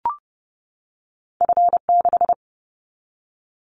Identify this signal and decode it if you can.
Morse code, heard as a keyed tone, about 30 wpm, 700 Hz. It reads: F6
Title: Callsign Prefixes for Top 50 DXCC - 30wpm